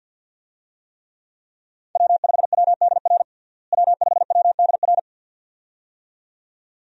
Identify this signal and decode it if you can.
Morse code, heard as a keyed tone, about 46 wpm, 700 Hz. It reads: W5WDR W5WDR
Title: USA Call Signs - 5 Characters - Repeated 2x - 50wpm